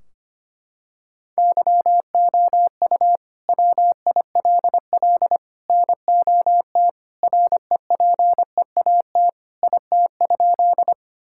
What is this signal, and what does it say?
Morse code, heard as a keyed tone, about 25 wpm, 700 Hz. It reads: YOU WILL NOT REPEAT IT?
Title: Sentences from Top 1600 Words 25wpm